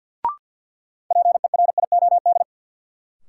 Morse code, heard as a keyed tone, about 53 wpm, 700 Hz. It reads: PERIOD
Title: Sets of 5 Words - New Word Drill - 50wpm